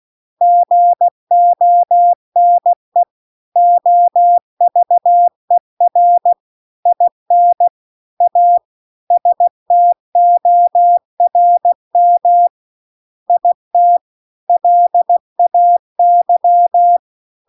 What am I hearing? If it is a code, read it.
Morse code, heard as a keyed tone, about 16 wpm, 700 Hz. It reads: GONE OVER IN A STORM IT LAY
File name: CW-The-Big-Twoheaded-River-2-16WPM